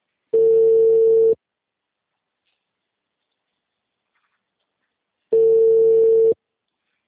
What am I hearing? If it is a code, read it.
Ringback tone (China, 450 Hz).